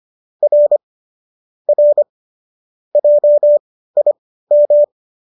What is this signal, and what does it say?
Morse code, heard as a keyed tone, about 25 wpm, spaced as for 15 wpm, 590 Hz. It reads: R R JIM